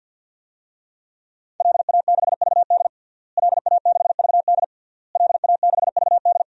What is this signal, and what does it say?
Morse code, heard as a keyed tone, about 51 wpm, 700 Hz. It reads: LA6VD LA6VD LA6VD
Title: DXCC-100 Call Signs - 5 Characters - Repeated 3x - 50wpm